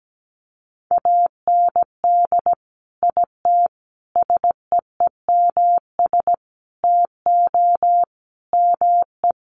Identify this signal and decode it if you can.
Morse code, heard as a keyed tone, about 17 wpm, 700 Hz. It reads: AND IT SEEMS TO ME
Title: Sets of 5 Words Encore2 17wpm